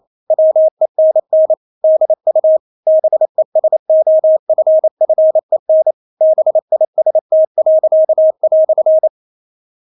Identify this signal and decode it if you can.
Morse code, heard as a keyed tone, about 28 wpm, 630 Hz. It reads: WENN DU BESOFFEN BIST."
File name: Witz9 28wpm